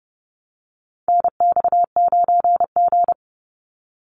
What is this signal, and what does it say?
Morse code, heard as a keyed tone, about 30 wpm, 700 Hz. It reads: NX9G
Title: Call Signs Repeated 3x - 30wpm